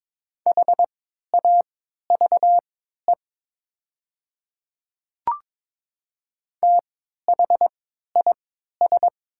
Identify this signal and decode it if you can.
Morse code, heard as a keyed tone, about 22 wpm, spaced as for 13 wpm, 690 Hz. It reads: HAVE THIS
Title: 3x Character Spacing - Top 100 Words - 22wpm